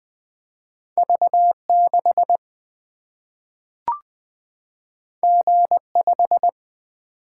Morse code, heard as a keyed tone, about 20 wpm, 700 Hz. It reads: V 6 G 5
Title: Single-Letter Single-Number 20wpm